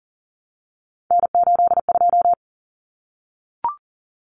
Morse code, heard as a keyed tone, about 40 wpm, 700 Hz. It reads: N82
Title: Common Cut Numbers - 3-Digit 40wpm